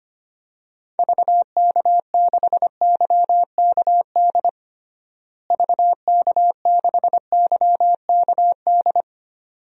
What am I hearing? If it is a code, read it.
Morse code, heard as a keyed tone, about 25 wpm, 700 Hz. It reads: VK6YKD VK6YKD
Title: DXCC-100 Call Signs - 6 Characters - Repeated 2x - 25wpm